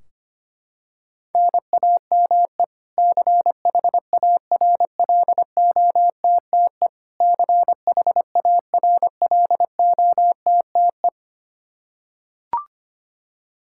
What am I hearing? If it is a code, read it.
Morse code, heard as a keyed tone, about 25 wpm, 700 Hz. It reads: NAME CHARLOTTE CHARLOTTE
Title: Practice QSO - Exchange 1 - Beginning - 25wpm